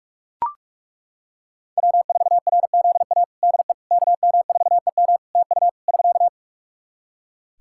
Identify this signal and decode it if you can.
Morse code, heard as a keyed tone, about 42 wpm, 700 Hz. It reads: W4RZA DE KM4EM TU <SK>
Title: Practice QSO - Exchange 4 - Ending - 45wpm